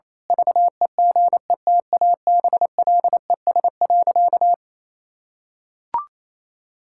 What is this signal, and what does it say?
Morse code, heard as a keyed tone, about 28 wpm, 700 Hz. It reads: VEGETABLES.
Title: Pangram Challenge v2 - 28wpm